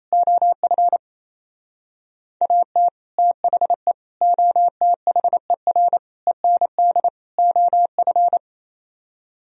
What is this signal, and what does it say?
Morse code, heard as a keyed tone, about 32 wpm, 700 Hz. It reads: OF AT THE OTHER END OF
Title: Speed-Racing - Sets of 5 Words 28wpm